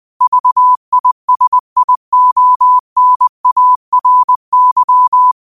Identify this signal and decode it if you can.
Morse code, heard as a keyed tone, about 20 wpm, 1000 Hz. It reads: VISIONARY